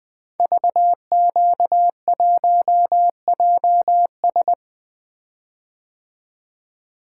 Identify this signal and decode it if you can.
Morse code, heard as a keyed tone, about 20 wpm, 700 Hz. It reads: VQ1JS
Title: Code Groups - 5 Letter:Numbers - 20wpm